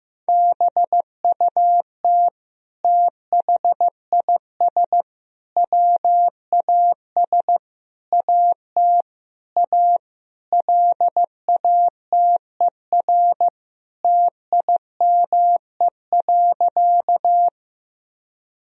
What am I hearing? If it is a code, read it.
Morse code, heard as a keyed tone, about 15 wpm, 700 Hz. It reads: BUT THIS WAS AT A LATER TIME.
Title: Sentences from Top 1000 Words - Review Part 4 - 15wpm